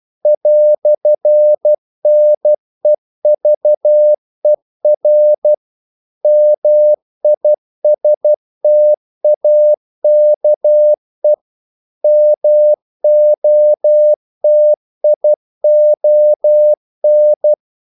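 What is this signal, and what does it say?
Morse code, heard as a keyed tone, about 12 wpm, 600 Hz. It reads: "NEVER MISTAKE MOTION
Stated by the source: Morse practice sending English quotations.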